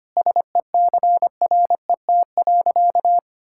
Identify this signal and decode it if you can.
Morse code, heard as a keyed tone, about 25 wpm, 700 Hz. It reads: SECRET.
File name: practice / 25wpm / Sentences with 3-Word Phrases - 25wpm